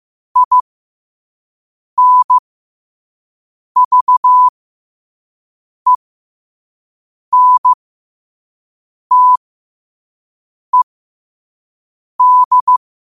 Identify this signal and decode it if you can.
Morse code, heard as a keyed tone, about 15 wpm, spaced as for 5 wpm, 1000 Hz. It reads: INVENTED